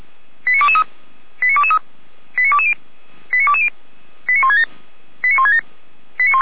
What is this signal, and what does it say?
Five-tone selective call (ZVEI), heard as quick runs of five tones, tones of about 70 ms. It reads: 89223 89223 89220 89220 89167 89167